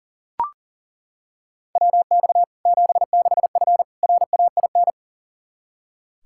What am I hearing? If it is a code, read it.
Morse code, heard as a keyed tone, about 40 wpm, 700 Hz. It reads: WX 76F RAIN
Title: Practice QSO - Exchange 2 - Info - 40wpm